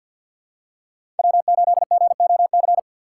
Morse code, heard as a keyed tone, about 48 wpm, 700 Hz. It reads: W8GOC
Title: USA Call Signs - 5 Characters - 3x Character Spacing - 50wpm